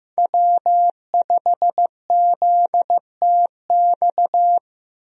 Morse code, heard as a keyed tone, about 15 wpm, 700 Hz. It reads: W5ZTX
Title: USA Call Signs - 5 Characters - Repeated 2x - 15wpm